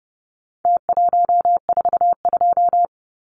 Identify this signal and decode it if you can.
Morse code, heard as a keyed tone, about 30 wpm, 700 Hz. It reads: T142
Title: Numbers - Contest Style - 4-Digit 30wpm